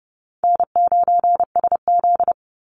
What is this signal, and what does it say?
Morse code, heard as a keyed tone, about 30 wpm, 700 Hz. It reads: N9SZ
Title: Easy Call Signs Repeated 3x - 30wpm